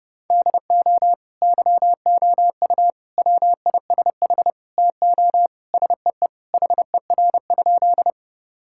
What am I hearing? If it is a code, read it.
Morse code, heard as a keyed tone, about 28 wpm, 700 Hz. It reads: DO YOU WISH TO SEE HER?
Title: Sentences from Top 1600 Words 30wpm